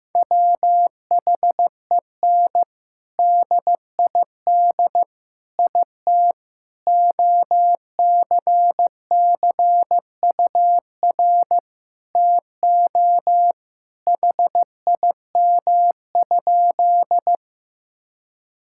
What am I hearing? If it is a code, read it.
Morse code, heard as a keyed tone, about 16 wpm, 700 Hz. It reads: WHEN DID IT OCCUR TO HIM?
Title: Sentences from Top 1000 Words - Review Part 6 - 15wpm